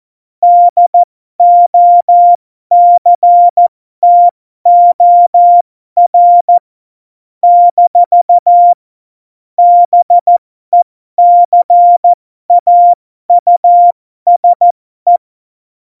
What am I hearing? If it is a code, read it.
Morse code, heard as a keyed tone, about 14 wpm, spaced as for 12 wpm, 700 Hz. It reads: DOCTOR - BECAUSE
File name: Morsekurs / Witze